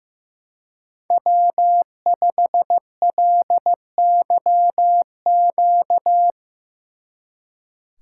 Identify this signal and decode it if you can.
Morse code, heard as a keyed tone, about 16 wpm, 700 Hz. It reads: W5LYQ
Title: Speed Racing - USA Call Signs 15wpm